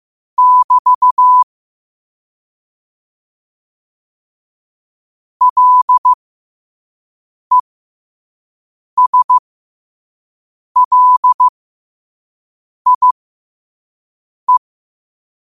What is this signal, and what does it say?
Morse code, heard as a keyed tone, about 15 wpm, spaced as for 5 wpm, 1000 Hz. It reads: <BT> LESLIE
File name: qotdfiles / 5wpm